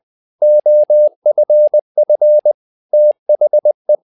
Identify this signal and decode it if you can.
Morse code, heard as a keyed tone, about 20 wpm, 600 Hz. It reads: OFF THE
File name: tower trouble 20wpm